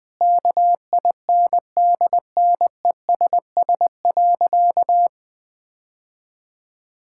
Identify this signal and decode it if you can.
Morse code, heard as a keyed tone, about 20 wpm, 700 Hz. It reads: KINDNESS.
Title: Sentences with 7-Word Phrases - 20wpm